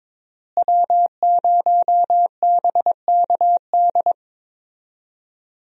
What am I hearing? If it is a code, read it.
Morse code, heard as a keyed tone, about 22 wpm, 700 Hz. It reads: W0BKD
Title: USA Call Signs - 5 Characters - Repeated 2x - 22wpm